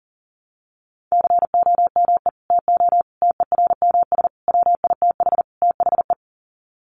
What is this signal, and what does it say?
Morse code, heard as a keyed tone, about 40 wpm, 700 Hz. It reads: COME TO TERMS WITH THE
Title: Sets of 5 Words Encore2 40wpm